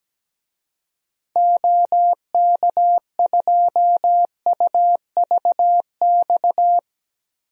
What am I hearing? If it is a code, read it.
Morse code, heard as a keyed tone, about 17 wpm, 700 Hz. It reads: OK2UVX
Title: DXCC-100 Call Signs - 6 Characters - Repeated 3x - 17wpm